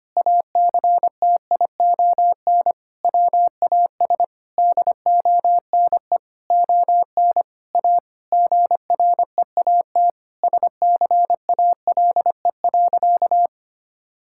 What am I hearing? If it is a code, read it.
Morse code, heard as a keyed tone, about 25 wpm, 700 Hz. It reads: ACTION WAS DONE ON A GREAT SCALE.
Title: Sentences from Top 1000 Words - Review Part 5 - 25wpm